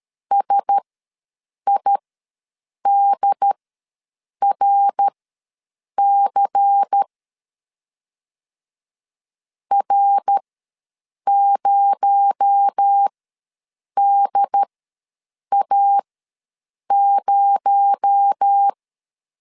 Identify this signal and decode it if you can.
Morse code, heard as a keyed tone, about 13 wpm, spaced as for 7 wpm, 800 Hz. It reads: SIDRC R0DA0
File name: LEZIONI / 6 PKX0 / 6 PKX0 ws 12wpm 3448